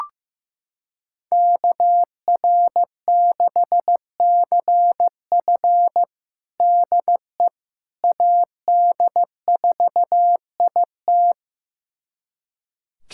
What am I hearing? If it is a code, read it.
Morse code, heard as a keyed tone, about 16 wpm, 700 Hz. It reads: KR6CF DE AD4IT
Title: USA Callsign DE Callsign - 5 Characters - 15wpm